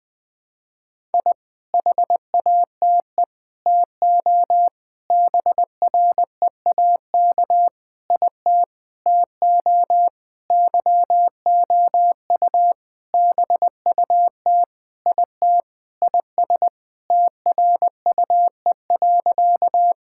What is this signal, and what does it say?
Morse code, heard as a keyed tone, about 21 wpm, 700 Hz. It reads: I HATE TO BREAK IT TO YOU BUT IT IS TRUE.
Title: Sentences with 8-Word Phrases - 20wpm